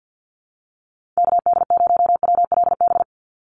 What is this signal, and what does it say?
Morse code, heard as a keyed tone, about 50 wpm, 700 Hz. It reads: KD0WLB
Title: USA Call Signs 50wpm